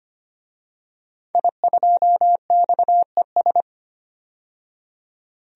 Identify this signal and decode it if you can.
Morse code, heard as a keyed tone, about 25 wpm, 700 Hz. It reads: I2XES
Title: DXCC-100 Call Signs - 5 Characters - 25wpm